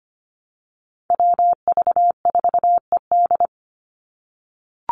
Morse code, heard as a keyed tone, about 25 wpm, 700 Hz. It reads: WV4ED
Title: USA Call Signs 25wpm